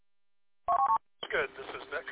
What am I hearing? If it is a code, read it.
Speech. There is no code.